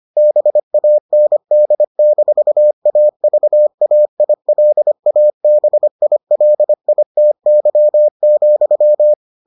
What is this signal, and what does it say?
Morse code, heard as a keyed tone, about 25 wpm, 600 Hz. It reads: BAND-AVAILABILITY,